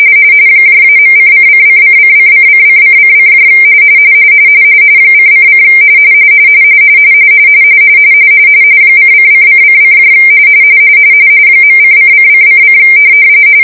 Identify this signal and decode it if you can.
RTTY, about 45.45 baud. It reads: -.::::::::::::: ::: :::::: .::::::